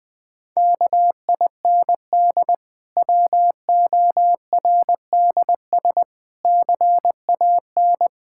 Morse code, heard as a keyed tone, about 20 wpm, 700 Hz. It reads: KIND WORDS CAN
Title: Difficult Sentences with Hints - 20wpm